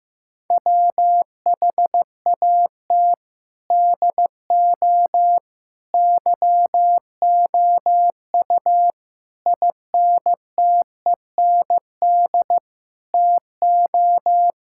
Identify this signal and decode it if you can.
Morse code, heard as a keyed tone, about 16 wpm, 700 Hz. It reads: WHAT DO YOU INTEND TO
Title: Sentences from Top 1600 Words 15wpm